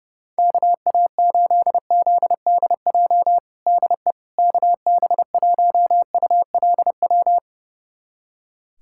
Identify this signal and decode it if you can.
Morse code, heard as a keyed tone, about 32 wpm, 700 Hz. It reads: KA8ZDJ DE KB1ULW
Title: USA Callsign DE Callsign 30wpm